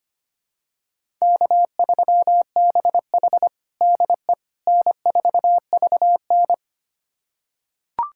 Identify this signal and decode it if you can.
Morse code, heard as a keyed tone, about 25 wpm, 700 Hz. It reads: K3BH DE N4VN
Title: USA Callsign DE Callsign - 4 Characters - 25wpm